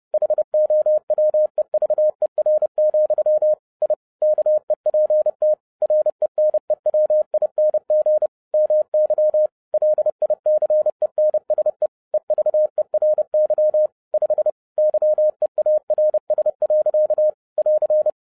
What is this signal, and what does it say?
Morse code, heard as a keyed tone, about 30 wpm, 610 Hz. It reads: HOWEVER, I KEPT RENEWING MY LICENSE EVERY 5 YEARS. <AR>
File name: MY CW LIFE 1 30wpm